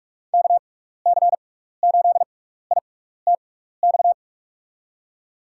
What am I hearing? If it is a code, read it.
Morse code, heard as a keyed tone, about 42 wpm, spaced as for 15 wpm, 700 Hz. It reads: KC8ITX
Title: USA Call Signs - 6x Character Spacing - 45wpm